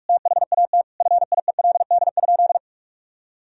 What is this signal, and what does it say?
Morse code, heard as a keyed tone, about 46 wpm, 700 Hz. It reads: THAT FIELD?
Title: Sentences from Top 1000 Words - Review Part 2 - 45wpm